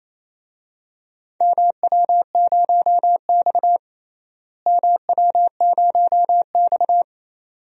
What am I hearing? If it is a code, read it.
Morse code, heard as a keyed tone, about 28 wpm, 700 Hz. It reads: MW0X MW0X
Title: DXCC-100 Call Signs - 4 Characters - Repeated 2x - 28wpm